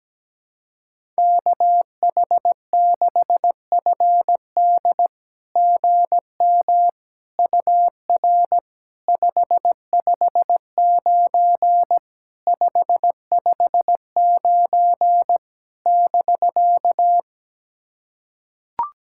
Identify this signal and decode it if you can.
Morse code, heard as a keyed tone, about 17 wpm, 690 Hz. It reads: KH6FD GM UR 559 559 <BK>
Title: SOTA QSO Practice 17wpm